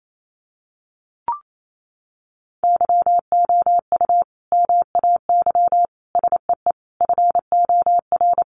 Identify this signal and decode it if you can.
Morse code, heard as a keyed tone, about 28 wpm, 700 Hz. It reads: YOU MAY SEE FOR
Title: Sentences from Top 600 Words 28wpm